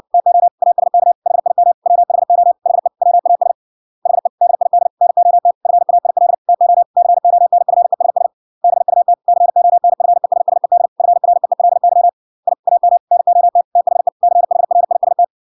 Morse code, heard as a keyed tone, about 56 wpm, 700 Hz. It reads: TO ASK HIM WHO HE WAS HE DID NOT FAIL TO CONFESS BUT CONFESSED FREELY I AM NOT THE CHRIST